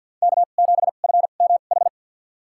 Morse code, heard as a keyed tone, about 51 wpm, spaced as for 36 wpm, 700 Hz. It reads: K7FMH
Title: USA Call Signs - 5 Characters - 2x Character Spacing - 50wpm